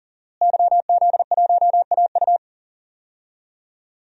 Morse code, heard as a keyed tone, about 40 wpm, 700 Hz. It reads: YZ1AU
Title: DXCC-100 Call Signs - 5 Characters - Repeated 2x - 40wpm